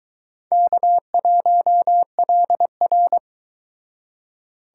Morse code, heard as a keyed tone, about 24 wpm, 700 Hz. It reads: K1LR